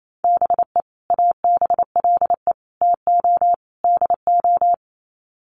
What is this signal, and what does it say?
Morse code, heard as a keyed tone, about 28 wpm, 700 Hz. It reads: BE ABLE TO DO
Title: Sets of 4 Words 28wpm